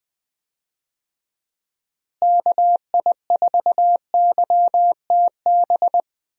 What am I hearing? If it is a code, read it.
Morse code, heard as a keyed tone, about 21 wpm, 700 Hz. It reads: KI4YTB
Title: USA Call Signs - 6 Characters - Repeated 2x - 20wpm